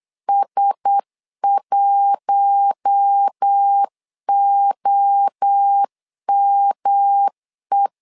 Morse code, heard as a keyed tone, about 8 wpm, 800 Hz. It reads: S1OME
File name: LEZIONI / 4 Q1S 8wpm 2710